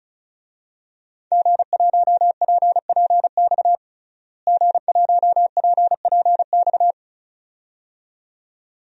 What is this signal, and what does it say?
Morse code, heard as a keyed tone, about 36 wpm, 700 Hz. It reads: G1PPX G1PPX
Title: DXCC-100 Call Signs - 5 Characters - Repeated 2x - 35wpm